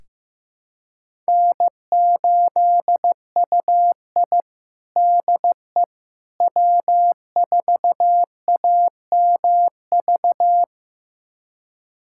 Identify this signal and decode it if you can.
Morse code, heard as a keyed tone, about 15 wpm, 700 Hz. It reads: N8UI DE W4AMV
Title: USA Callsign DE Callsign 15wpm